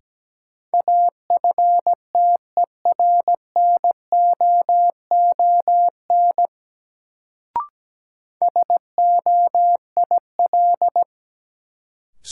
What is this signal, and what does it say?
Morse code, heard as a keyed tone, about 17 wpm, 700 Hz. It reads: AFTERNOON SOIL
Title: Sentences from Top 1500 Words - New Word Drill Alt - 17wpm